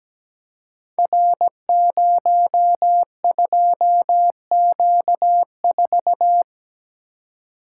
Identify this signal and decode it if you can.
Morse code, heard as a keyed tone, about 17 wpm, 700 Hz. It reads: R02Q4
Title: Code Groups - 5 Letter:Numbers - 17wpm